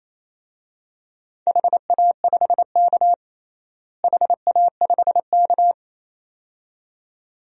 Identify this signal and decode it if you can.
Morse code, heard as a keyed tone, about 28 wpm, 700 Hz. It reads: HA5K HA5K
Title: DXCC-100 Call Signs - 4 Characters - Repeated 2x - 28wpm